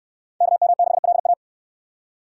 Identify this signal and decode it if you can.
Morse code, heard as a keyed tone, about 68 wpm, 690 Hz. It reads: KM6LA